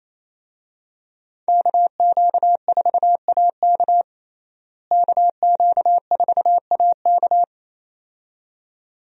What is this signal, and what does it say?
Morse code, heard as a keyed tone, about 28 wpm, 700 Hz. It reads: KQ4AK KQ4AK
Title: USA Call Signs - 5 Characters - Repeated 2x - 28wpm